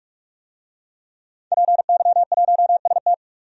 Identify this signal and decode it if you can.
Morse code, heard as a keyed tone, about 44 wpm, 700 Hz. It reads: PY1ST
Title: DXCC-100 Call Signs - 5 Characters - 45wpm